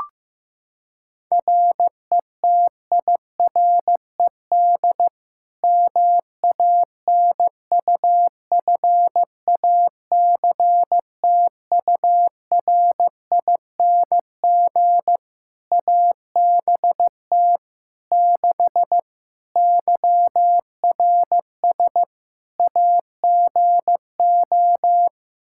Morse code, heard as a keyed tone, about 15 wpm, 700 Hz. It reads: RETIRED MANUFACTURING ABT 6 YRS AGO
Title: Practice QSO - Exchange 3 - More Info - 15wpm